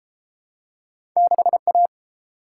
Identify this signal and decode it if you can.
Morse code, heard as a keyed tone, about 32 wpm, 700 Hz. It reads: 6A